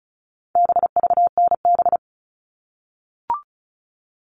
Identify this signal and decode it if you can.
Morse code, heard as a keyed tone, about 34 wpm, 700 Hz. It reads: BVNB
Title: All Cut Numbers - 4-Digit 35wpm